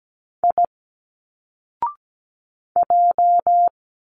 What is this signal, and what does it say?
Morse code, heard as a keyed tone, about 17 wpm, 700 Hz. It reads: I J
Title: Single Letters 17wpm